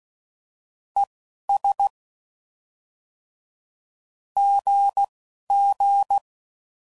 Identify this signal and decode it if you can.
Morse code, heard as a keyed tone, about 16 wpm, 800 Hz. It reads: E S G G